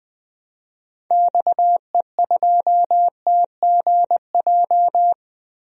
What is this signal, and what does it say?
Morse code, heard as a keyed tone, about 20 wpm, 700 Hz. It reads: XE2TGJ
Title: DXCC-100 Call Signs - 6 Characters - Repeated 2x - 20wpm